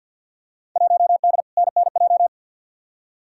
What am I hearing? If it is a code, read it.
Morse code, heard as a keyed tone, about 47 wpm, 700 Hz. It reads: 1D NNJ
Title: Field Day Exchanges - 50wpm